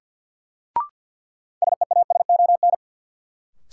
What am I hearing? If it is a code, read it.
Morse code, heard as a keyed tone, about 47 wpm, 700 Hz. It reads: SEASON
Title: Sentences from Top 1400 Words - New Word Drill - 50wpm